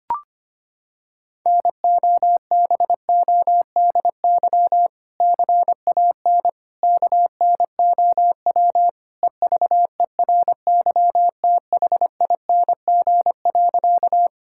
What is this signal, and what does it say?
Morse code, heard as a keyed tone, about 25 wpm, 700 Hz. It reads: NOBODY CAN KNOW EVERYTHING.
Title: Sentences from Top 1600 Words 25wpm